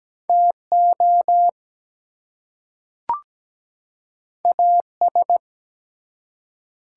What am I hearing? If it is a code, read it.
Morse code, heard as a keyed tone, about 17 wpm, 700 Hz. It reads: TO AS
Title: Two Letter Words - 17wpm